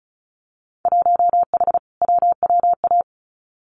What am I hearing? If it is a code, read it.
Morse code, heard as a keyed tone, about 36 wpm, 700 Hz. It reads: 1H WWA
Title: Winter Field Day Exchanges 35wpm